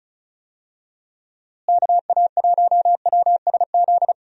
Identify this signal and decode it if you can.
Morse code, heard as a keyed tone, about 34 wpm, 700 Hz. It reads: KA1WSZ
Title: USA Call Signs - 6 Characters - Repeated 2x - 35wpm